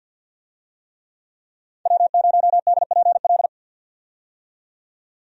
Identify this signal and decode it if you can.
Morse code, heard as a keyed tone, about 52 wpm, 700 Hz. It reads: W0DPL
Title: USA Call Signs - 5 Characters - 50wpm